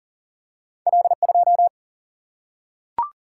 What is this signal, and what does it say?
Morse code, heard as a keyed tone, about 40 wpm, 700 Hz. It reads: L 2
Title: Single-Letter Single-Number 40wpm